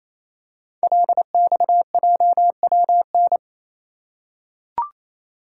Morse code, heard as a keyed tone, about 27 wpm, 700 Hz. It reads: LXJWN
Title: Code Groups - 5 Letter:Numbers - 28wpm